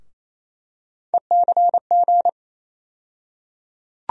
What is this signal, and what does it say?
Morse code, heard as a keyed tone, about 27 wpm, 700 Hz. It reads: ECG
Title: USA Call Sign Postfixes - 3 Letters 28wpm